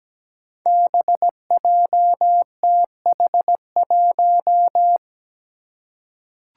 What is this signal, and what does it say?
Morse code, heard as a keyed tone, about 17 wpm, 700 Hz. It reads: BJTH1
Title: Five Challenging Letter-Number Combinations 17wpm